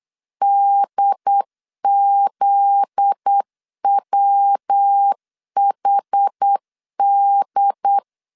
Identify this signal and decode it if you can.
Morse code, heard as a keyed tone, about 8 wpm, 800 Hz. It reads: DZWHD
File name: LEZIONI / 8 YJ45 / 8 YJ4 8wpm 3246